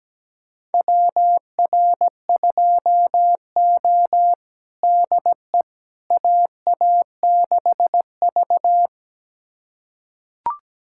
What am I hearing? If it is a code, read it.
Morse code, heard as a keyed tone, about 17 wpm, 700 Hz. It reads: WR2O DE AA6V
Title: USA Callsign DE Callsign - 4 Characters - 17wpm